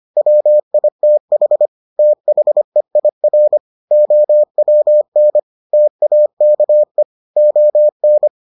Morse code, heard as a keyed tone, about 25 wpm, 600 Hz. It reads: WITH THEIR OWN TAKE ON